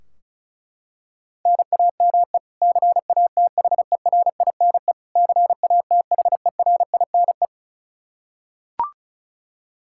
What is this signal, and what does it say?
Morse code, heard as a keyed tone, about 36 wpm, 700 Hz. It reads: NAME CATHERINE CATHERINE
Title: Practice QSO - Exchange 1 - Beginning - 35wpm